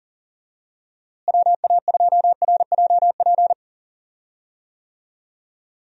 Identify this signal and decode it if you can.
Morse code, heard as a keyed tone, about 40 wpm, 700 Hz. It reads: WA2RJP
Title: USA Call Signs - 6 Characters - Repeated 2x - 40wpm